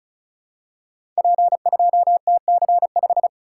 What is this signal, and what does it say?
Morse code, heard as a keyed tone, about 35 wpm, 690 Hz. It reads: P2TC5
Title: Code Groups - 5 Letter:Numbers - 35wpm